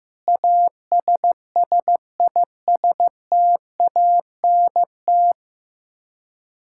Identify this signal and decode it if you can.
Morse code, heard as a keyed tone, about 15 wpm, 700 Hz. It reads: ASSISTANT
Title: Practice QSO - Exchange 3 - More Info - 15wpm